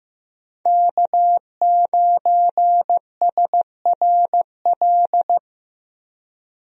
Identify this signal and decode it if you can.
Morse code, heard as a keyed tone, about 15 wpm, 700 Hz. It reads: K9SRL
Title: USA Call Signs - 5 Characters - Repeated 2x - 15wpm